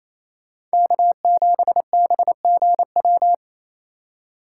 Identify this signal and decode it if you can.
Morse code, heard as a keyed tone, about 36 wpm, 700 Hz. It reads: K7BGW